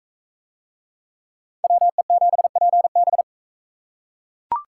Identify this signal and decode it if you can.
Morse code, heard as a keyed tone, about 39 wpm, 700 Hz. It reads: WE7PB